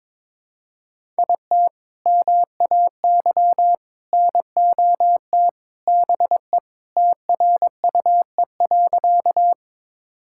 Morse code, heard as a keyed tone, about 22 wpm, 700 Hz. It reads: IT MAY NOT BE TRUE.
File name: practice / 22wpm / Sentences with 3-Word Phrases - 22wpm